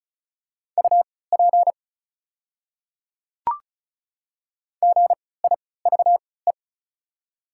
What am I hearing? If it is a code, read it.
Morse code, heard as a keyed tone, about 36 wpm, spaced as for 20 wpm, 700 Hz. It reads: UP GIVE
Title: 3x Character Spacing - Top 200 Words - 35wpm